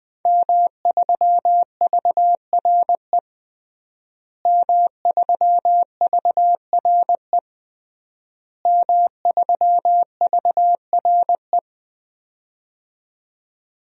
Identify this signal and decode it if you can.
Morse code, heard as a keyed tone, about 21 wpm, 700 Hz. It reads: M3VRE M3VRE M3VRE
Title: DXCC-100 Call Signs - 5 Characters - Repeated 3x - 20wpm